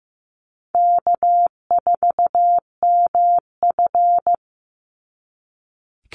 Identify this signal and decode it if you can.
Morse code, heard as a keyed tone, about 15 wpm, 700 Hz. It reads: K4MF
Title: NA SOTA Chaser Callsigns 15wpm